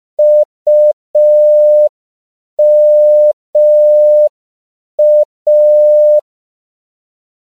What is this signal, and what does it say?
Morse code, heard as a keyed tone, about 5 wpm, 600 Hz. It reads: UMA
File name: CW / 5PPM